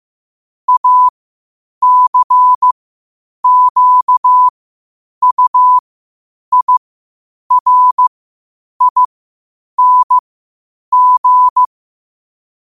Morse code, heard as a keyed tone, about 15 wpm, spaced as for 8 wpm, 1000 Hz. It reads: ACQUIRING
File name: qotdfiles / 8wpm